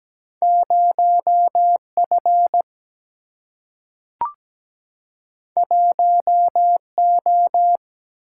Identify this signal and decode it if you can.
Morse code, heard as a keyed tone, about 17 wpm, 700 Hz. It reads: 0 F 1 O
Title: Single-Letter Single-Number 17wpm